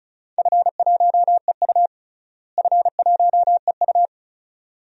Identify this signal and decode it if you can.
Morse code, heard as a keyed tone, about 36 wpm, 690 Hz. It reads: F1EU F1EU
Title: DXCC-100 Call Signs - 4 Characters - Repeated 2x - 35wpm